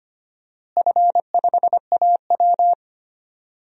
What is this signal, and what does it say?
Morse code, heard as a keyed tone, about 25 wpm, 700 Hz. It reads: F5AW
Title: DXCC-100 Call Signs - 4 Characters - Repeated 3x - 25wpm